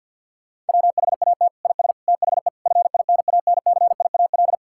Morse code, heard as a keyed tone, about 52 wpm, 700 Hz. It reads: WHAT IS THE FINANCIAL
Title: Sentences from Top 1000 Words - Review Part 5 - 50wpm